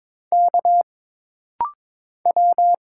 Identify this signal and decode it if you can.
Morse code, heard as a keyed tone, about 22 wpm, 700 Hz. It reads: K W
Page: This rapid-fire timing is geared to push your ICR (Instant Character Recognition) skills.